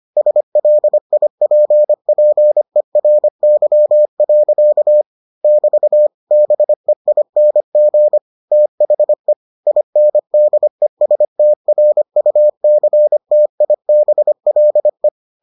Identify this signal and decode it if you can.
Morse code, heard as a keyed tone, about 25 wpm, 600 Hz. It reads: SLIPPERY. <BT> BEING THE INDESTRUCTIBLE